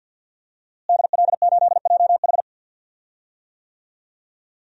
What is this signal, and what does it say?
Morse code, heard as a keyed tone, about 53 wpm, 700 Hz. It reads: DL8JH
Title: DXCC-100 Call Signs - 5 Characters - Repeated 3x - 50wpm